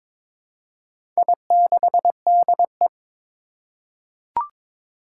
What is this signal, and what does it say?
Morse code, heard as a keyed tone, about 22 wpm, 700 Hz. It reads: I6DE
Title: Four Challenging Letter-Number Combinations 22wpm